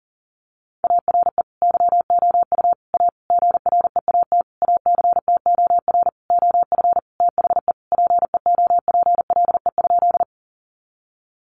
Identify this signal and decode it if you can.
Morse code, heard as a keyed tone, about 40 wpm, 700 Hz. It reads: ARE YOU A GREAT ACTOR OF THE PEOPLE?
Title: Sentences from Top 1500 Words 40wpm